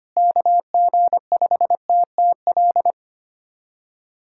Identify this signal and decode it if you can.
Morse code, heard as a keyed tone, about 25 wpm, 700 Hz. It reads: KG5TTL
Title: USA Callsign DE Callsign 25wpm